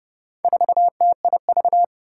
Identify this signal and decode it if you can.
Morse code, heard as a keyed tone, about 30 wpm, 700 Hz. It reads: 4TIV
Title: Four Challenging Letter-Number Combinations 30wpm